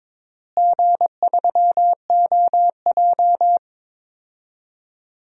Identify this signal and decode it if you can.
Morse code, heard as a keyed tone, about 22 wpm, 700 Hz. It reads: G3OJ
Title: DXCC-100 Call Signs - 4 Characters - Repeated 2x - 22wpm